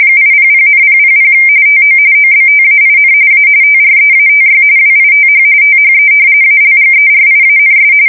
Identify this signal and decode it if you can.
RTTY, about 45.45 baud. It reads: 800,000 TO OBTAIN A TREATY WITH THE ALGERIANS
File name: fsk170 / usnavy1